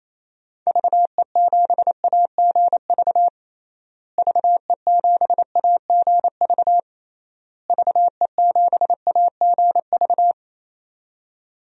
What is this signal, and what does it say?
Morse code, heard as a keyed tone, about 28 wpm, 700 Hz. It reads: VE7AGV VE7AGV VE7AGV
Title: DXCC-100 Call Signs - 6 Characters - Repeated 3x - 28wpm